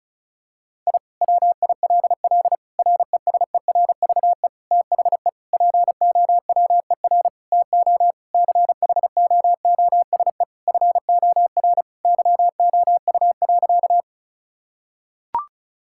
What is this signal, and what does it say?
Morse code, heard as a keyed tone, about 36 wpm, 700 Hz. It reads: I WILL RESERVE THE POWER TO CHOOSE FOR YOU.
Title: Sentences from Top 1000 Words - Review Part 6 - 35wpm